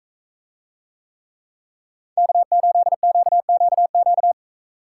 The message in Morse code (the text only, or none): K8QQQ